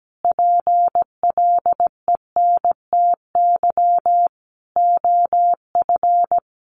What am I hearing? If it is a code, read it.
Morse code, heard as a keyed tone, about 17 wpm, 700 Hz. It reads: PLENTY OF
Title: Sentences from Top 1300 Words 17wpm